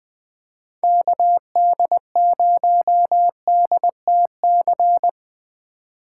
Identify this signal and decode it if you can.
Morse code, heard as a keyed tone, about 20 wpm, 700 Hz. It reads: KD0DTC
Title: USA Call Signs - 6 Characters - 6x Character Spacing - 20wpm